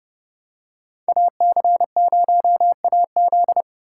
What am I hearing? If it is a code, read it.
Morse code, heard as a keyed tone, about 30 wpm, 700 Hz. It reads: AC0AZ